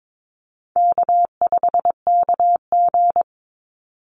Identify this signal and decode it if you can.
Morse code, heard as a keyed tone, about 22 wpm, 700 Hz. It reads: K5KG
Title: Easy Call Signs Repeated 3x - 22wpm